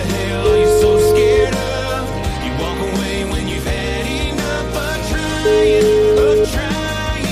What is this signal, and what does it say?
Ringback tone (Europe, 425 Hz), with noise.